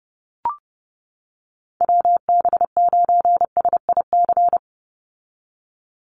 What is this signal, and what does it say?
Morse code, heard as a keyed tone, about 28 wpm, 700 Hz. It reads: WB9SIC
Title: USA Call Signs 30wpm